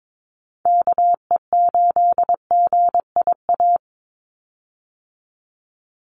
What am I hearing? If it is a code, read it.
Morse code, heard as a keyed tone, about 22 wpm, 700 Hz. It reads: KE8GIA
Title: USA Call Signs 22wpm